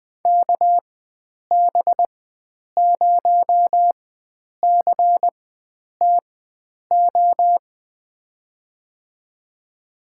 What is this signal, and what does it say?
Morse code, heard as a keyed tone, about 20 wpm, spaced as for 9 wpm, 700 Hz. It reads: KB0CTO
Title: USA Call Signs - 6 Characters - 4x Character Spacing - 20wpm